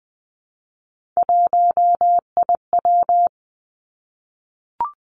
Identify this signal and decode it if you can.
Morse code, heard as a keyed tone, about 19 wpm, 700 Hz. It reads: 1IW
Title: Three Challenging Letter-Number Combinations 20wpm